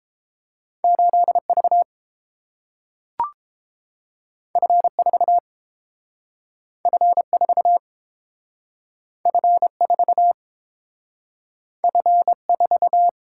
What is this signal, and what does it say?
Morse code, heard as a keyed tone, about 30 wpm, 700 Hz. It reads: 8V F4 F4 F4 F4